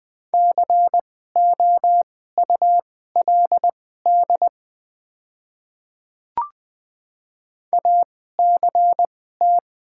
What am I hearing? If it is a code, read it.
Morse code, heard as a keyed tone, about 20 wpm, spaced as for 14 wpm, 700 Hz. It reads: COULD ACT
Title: 2x Character Spacing - Top 300 Words - 20wpm